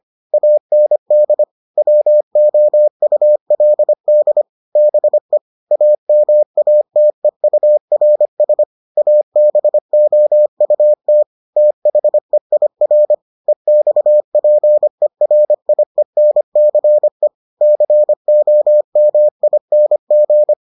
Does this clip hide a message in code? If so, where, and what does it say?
Yes: Morse code from 0.3 to 20.5 s, heard as a keyed tone, about 25 wpm, 600 Hz. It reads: AND WOULD BE AMATEURS ABOUT THEIR EXPERIENCE COMING